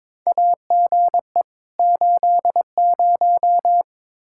Morse code, heard as a keyed tone, about 22 wpm, 700 Hz. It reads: AGE 80
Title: K1USN SST QSO Practice 22wpm